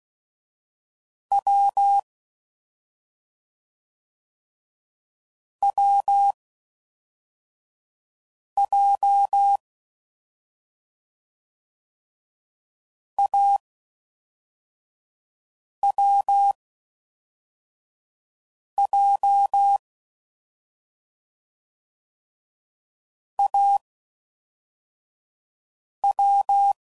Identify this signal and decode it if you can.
Morse code, heard as a keyed tone, about 16 wpm, 800 Hz. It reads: W W J A W J A W